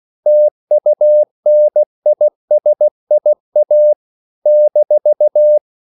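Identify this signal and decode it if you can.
Morse code, heard as a keyed tone, about 16 wpm, 600 Hz. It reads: TUNISIA -
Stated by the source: CW practice of national capitals at 16 words per minute, from ON6ZQ.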